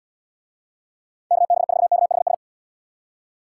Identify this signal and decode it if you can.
Morse code, heard as a keyed tone, about 70 wpm, 690 Hz. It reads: KB4KHS